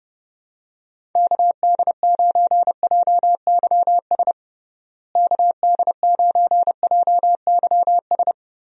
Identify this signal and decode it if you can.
Morse code, heard as a keyed tone, about 30 wpm, 700 Hz. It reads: KD9JYS KD9JYS
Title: USA Call Signs - 6 Characters - Repeated 3x - 30wpm